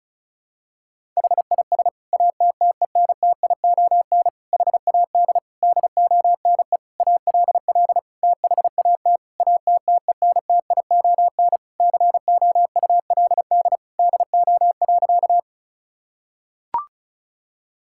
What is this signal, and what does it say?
Morse code, heard as a keyed tone, about 34 wpm, 700 Hz. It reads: HIS ATTENTION HAD DONE ALL THAT ATTENTION COULD DO.
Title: Sentences from Top 1000 Words - Review Part 6 - 35wpm